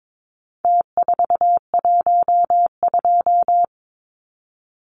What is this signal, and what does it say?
Morse code, heard as a keyed tone, about 22 wpm, 700 Hz. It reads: T412
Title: Numbers - Contest Style - 4-Digit 22wpm